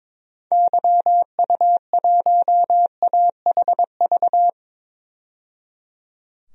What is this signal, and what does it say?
Morse code, heard as a keyed tone, about 22 wpm, 700 Hz. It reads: YU1AHV
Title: DXCC-100 Call Signs - 6 Characters - 22wpm